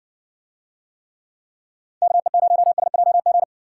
Morse code, heard as a keyed tone, about 62 wpm, 700 Hz. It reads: KE0SJG